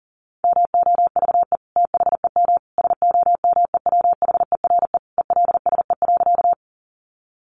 Morse code, heard as a keyed tone, about 40 wpm, 700 Hz. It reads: MOVE THEM SOMEWHERE ELSE.
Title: Sentences from Top 1400 Words 40wpm